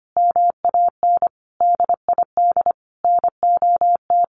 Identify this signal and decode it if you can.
Morse code, heard as a keyed tone, about 25 wpm, 700 Hz. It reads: MAN DID NOT
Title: A Sound Of Thunder 25wpm